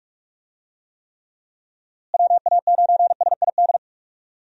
Morse code, heard as a keyed tone, about 44 wpm, 700 Hz. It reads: WA9SID
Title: USA Call Signs - 6 Characters - Repeated 2x - 45wpm